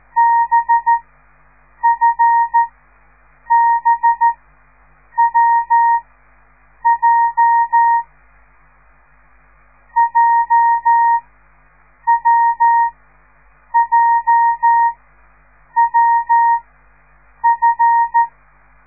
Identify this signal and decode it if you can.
Morse code, heard as a keyed tone, about 13 wpm, spaced as for 7 wpm, 930 Hz. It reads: BFBWJ JWJWF